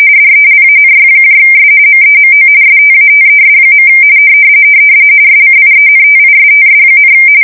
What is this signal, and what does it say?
RTTY, about 45.45 baud. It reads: DELANYS MEMO STATES. SCIENTISTS,
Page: 170 Shift 2125/2295